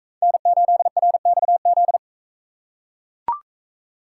Morse code, heard as a keyed tone, about 41 wpm, 700 Hz. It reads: N8RXZ